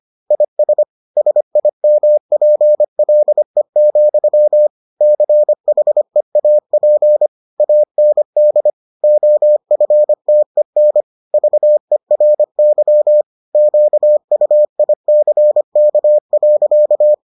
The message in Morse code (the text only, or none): IS SIMPLE, CHEAP AND OFTEN VERY QUICK.